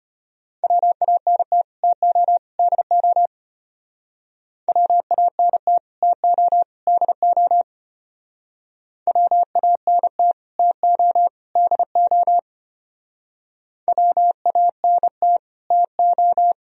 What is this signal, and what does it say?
Morse code, heard as a keyed tone, about 32 wpm, 700 Hz. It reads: WANT TO DO WANT TO DO WANT TO DO WANT TO